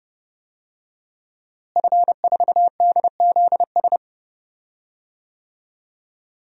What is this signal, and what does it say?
Morse code, heard as a keyed tone, about 30 wpm, 700 Hz. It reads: F4DZS
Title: DXCC-100 Call Signs - 5 Characters - Repeated 3x - 30wpm